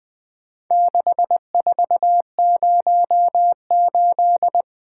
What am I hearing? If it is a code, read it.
Morse code, heard as a keyed tone, about 22 wpm, 700 Hz. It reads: 6408
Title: Speed-Racing - Four Letter-Number Combinations - 20wpm